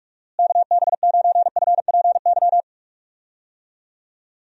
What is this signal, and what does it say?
Morse code, heard as a keyed tone, about 42 wpm, 700 Hz. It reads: KB9FPY
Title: USA Call Signs - 6 Characters - Repeated 3x - 45wpm